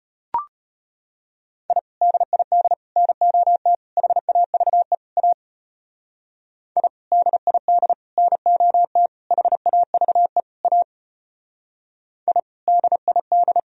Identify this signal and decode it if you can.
Morse code, heard as a keyed tone, about 31 wpm, 700 Hz. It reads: I DID NOT HAVE A I DID NOT HAVE A I DID
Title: Speed-Racing - Sets of 5 Words 25wpm